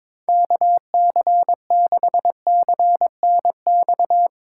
Morse code, heard as a keyed tone, about 22 wpm, 700 Hz. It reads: KC6CNX
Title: USA Call Signs - 6 Characters - 2x Character Spacing - 22wpm